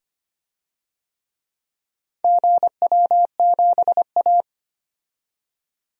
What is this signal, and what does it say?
Morse code, heard as a keyed tone, about 25 wpm, 700 Hz. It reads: GW7A
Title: DXCC-100 Call Signs - 4 Characters - 25wpm